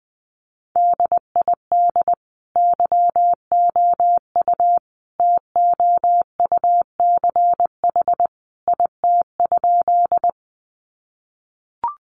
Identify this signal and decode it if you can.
Morse code, heard as a keyed tone, about 20 wpm, 700 Hz. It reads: DID YOU TOUCH IT?
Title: Sentences from Top 1200 Words 20wpm